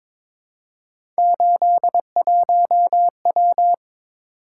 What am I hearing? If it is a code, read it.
Morse code, heard as a keyed tone, about 22 wpm, 700 Hz. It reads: 81W
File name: practice / 22wpm / Three Letter-Number Combinations - 22wpm